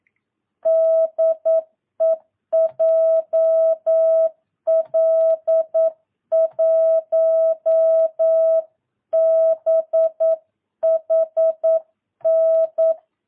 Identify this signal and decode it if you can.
Morse code, heard as a keyed tone, about 9 wpm, 650 Hz. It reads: DE JL1BHN
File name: jh8cbh / 2022cw